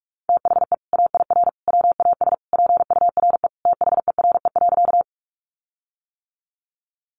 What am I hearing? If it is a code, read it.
Morse code, heard as a keyed tone, about 45 wpm, 700 Hz. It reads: THE AIR WAS PURE THERE.
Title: Sentences from Top 1500 Words 45wpm